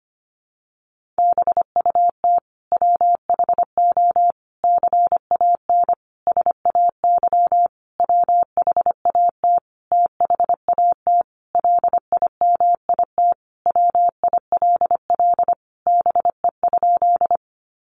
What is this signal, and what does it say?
Morse code, heard as a keyed tone, about 25 wpm, 700 Hz. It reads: BUT WHO CAN SAY WHAT THAT LIMIT WILL BE?
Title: Sentences from Top 1500 Words 25wpm